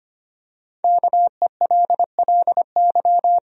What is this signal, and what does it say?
Morse code, heard as a keyed tone, about 25 wpm, 700 Hz. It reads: KELLY
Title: USA Common Names - Max 5 Letters - 25wpm